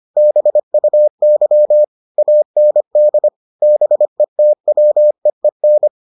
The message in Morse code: BUY AND BETWEEN

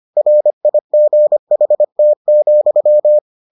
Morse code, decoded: RIGHT,

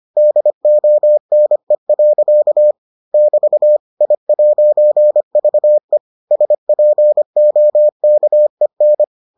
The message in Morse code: DONE. <BT> I'VE SPOKEN

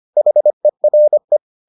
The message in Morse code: HERE

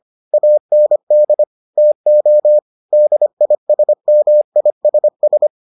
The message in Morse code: AND TO DISMISS

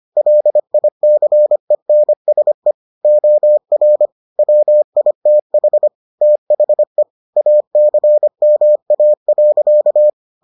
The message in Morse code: LICENSE OR WITH THE ACMA.

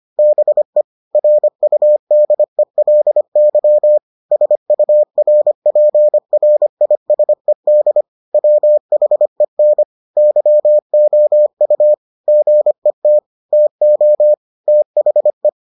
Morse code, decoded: BE RUDELY SURPRISED WHEN YOU GET TO THE

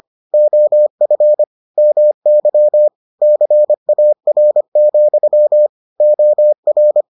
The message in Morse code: OF MY CAR, OR

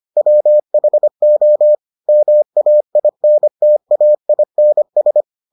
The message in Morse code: WHO MAINTAINS

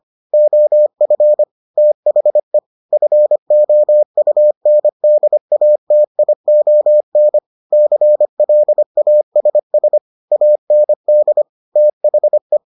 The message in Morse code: OF THE FOUNDATION CLASS AND THE

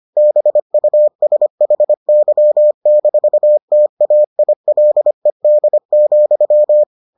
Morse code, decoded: BUSHY-TAILED,

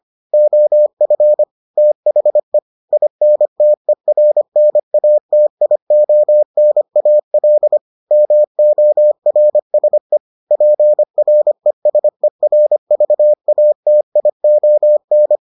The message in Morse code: OF THE INTERNATIONAL MORSE PRESERVATION

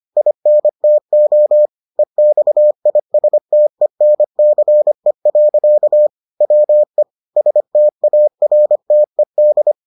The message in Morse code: INTO EXISTENCE. WE STARTED

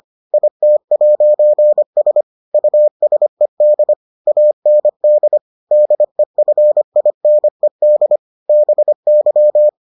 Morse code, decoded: IT'S USED AND DEFINED BY

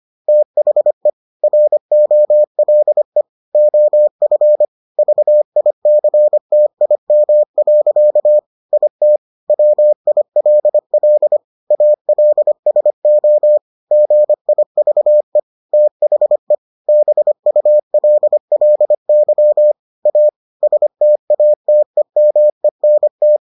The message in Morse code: THE ROLE OF VICTIM. IT WILL ALSO GIVE THE BULLY A STATEMENT

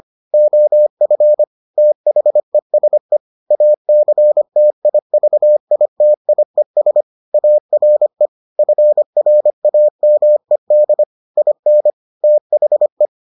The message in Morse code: OF THESE ACTIVITIES ARE FRAMED IN THE

